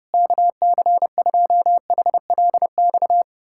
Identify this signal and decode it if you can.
Morse code, heard as a keyed tone, about 31 wpm, 700 Hz. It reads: KC2HLX
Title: USA Callsign DE Callsign 30wpm